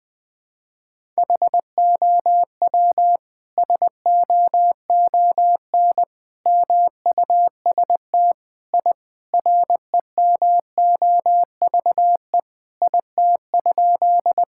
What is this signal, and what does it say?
Morse code, heard as a keyed tone, about 21 wpm, 700 Hz. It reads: HOW SOON MUST I REMOVE IT?
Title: Sentences from Top 1000 Words - Review Part 5 - 20wpm